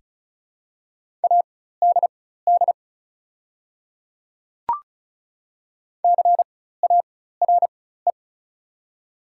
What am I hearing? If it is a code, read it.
Morse code, heard as a keyed tone, about 36 wpm, spaced as for 16 wpm, 700 Hz. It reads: ADD CARE
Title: 4x Character Spacing - Top 300 Words - 35wpm